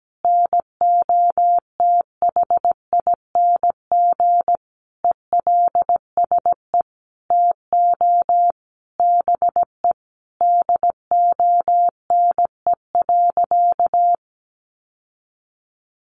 Morse code, heard as a keyed tone, about 17 wpm, 700 Hz. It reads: NOTHING ELSE TO BE DONE.
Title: Sentences from Top 800 Words 17wpm